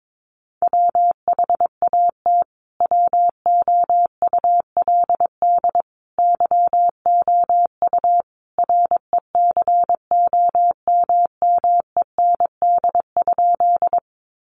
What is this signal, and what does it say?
Morse code, heard as a keyed tone, about 22 wpm, 700 Hz. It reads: WHAT WOULD YOU RECOMMEND?
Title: Sentences from Top 1200 Words 22wpm